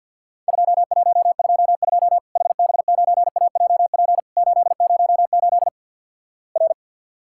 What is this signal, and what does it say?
Morse code, heard as a keyed tone, about 50 wpm, 700 Hz. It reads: 2122 HB9AJP 808 R
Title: Worked All Europe DX Contest - QTC Exchange - 50wpm